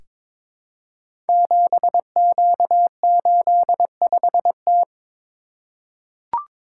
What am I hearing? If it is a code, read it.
Morse code, heard as a keyed tone, about 22 wpm, 700 Hz. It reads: 7Q85T
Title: Code Groups - 5 Letter:Numbers - 22wpm